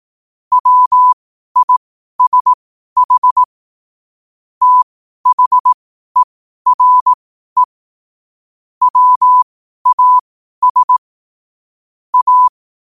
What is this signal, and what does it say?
Morse code, heard as a keyed tone, about 18 wpm, spaced as for 12 wpm, 1000 Hz. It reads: WISH THERE WAS A
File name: qotdfiles / 12wpm